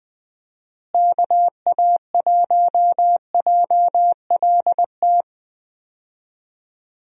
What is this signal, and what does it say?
Morse code, heard as a keyed tone, about 20 wpm, 700 Hz. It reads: KA1JLT
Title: USA Call Signs - 6 Characters - Repeated 2x - 20wpm